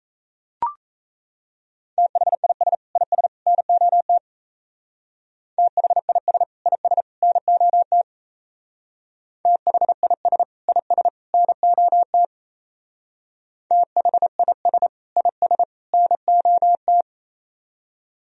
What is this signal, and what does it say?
Morse code, heard as a keyed tone, about 34 wpm, 700 Hz. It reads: THIS IS NOT THIS IS NOT THIS IS NOT THIS IS NOT
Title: Speed-Racing - Sets of 3 Words 28wpm